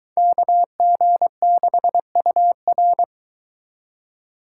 Morse code, heard as a keyed tone, about 23 wpm, 700 Hz. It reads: KG6UR